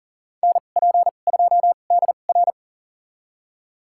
Morse code, heard as a keyed tone, about 37 wpm, spaced as for 28 wpm, 700 Hz. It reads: NP2DR
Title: USA Call Signs - 5 Characters - 2x Character Spacing - 40wpm